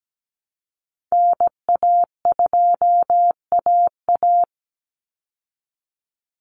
Morse code, heard as a keyed tone, about 17 wpm, 700 Hz. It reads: NA2AA
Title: Call Signs Repeated 2x - 17wpm